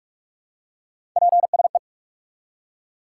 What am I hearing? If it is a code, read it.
Morse code, heard as a keyed tone, about 42 wpm, 700 Hz. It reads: PSE
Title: Top 90 QSO Elements 45wpm